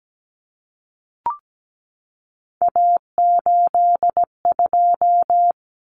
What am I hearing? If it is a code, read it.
Morse code, heard as a keyed tone, about 17 wpm, 700 Hz. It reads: A82
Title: Common Cut Numbers - 3-Digit 17wpm